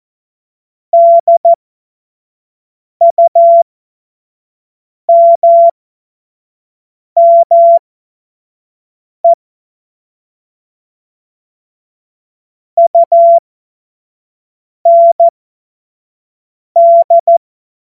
Morse code, heard as a keyed tone, about 14 wpm, spaced as for 5 wpm, 680 Hz. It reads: DUMME UND
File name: Witz-15-Juli-2025-25-F70